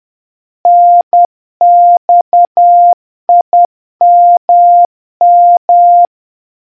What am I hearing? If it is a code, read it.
Morse code, heard as a keyed tone, about 10 wpm, 700 Hz. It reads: NXIMM